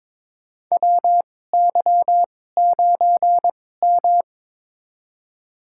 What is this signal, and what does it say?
Morse code, heard as a keyed tone, about 22 wpm, spaced as for 16 wpm, 700 Hz. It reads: WY9M
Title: USA Call Signs - 4 Characters - 2x Character Spacing - 22wpm